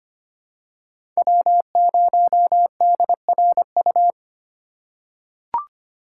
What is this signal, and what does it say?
Morse code, heard as a keyed tone, about 25 wpm, 700 Hz. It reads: W0DRU
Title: USA Call Signs - 5 Characters - 3x Character Spacing - 25wpm